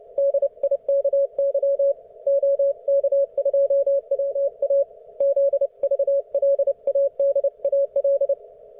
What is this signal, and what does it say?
Morse code, heard as a keyed tone, about 29 wpm, 560 Hz. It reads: DIKY OK2WA ZVLADAL